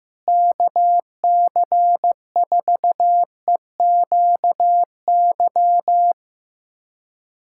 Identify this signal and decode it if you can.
Morse code, heard as a keyed tone, about 15 wpm, 700 Hz. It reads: KC4EQY
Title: USA Callsign DE Callsign 15wpm